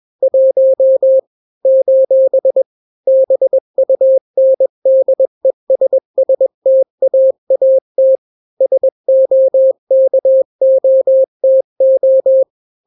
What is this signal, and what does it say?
Morse code, heard as a keyed tone, about 21 wpm, 530 Hz. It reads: 1 : BUNDESSTAAT SOKOTO
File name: daily / 20wpm